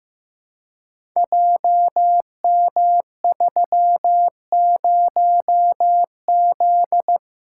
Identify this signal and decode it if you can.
Morse code, heard as a keyed tone, about 15 wpm, 700 Hz. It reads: JM30Z
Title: Code Groups - 5 Letter:Numbers - 15wpm